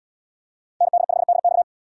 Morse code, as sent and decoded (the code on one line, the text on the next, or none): -. -.. ..... -.. -..-
ND5DX